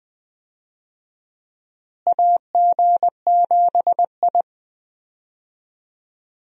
Morse code, heard as a keyed tone, about 20 wpm, 700 Hz. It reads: AG7I
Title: USA Call Signs - 4 Characters - Repeated 3x - 20wpm